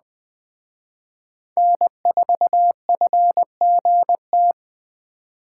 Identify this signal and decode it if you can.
Morse code, heard as a keyed tone, about 21 wpm, 700 Hz. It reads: N4FGT